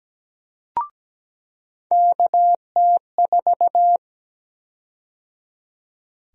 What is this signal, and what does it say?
Morse code, heard as a keyed tone, about 17 wpm, 700 Hz. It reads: KT4
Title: USA Call Sign Prefixes - 17wpm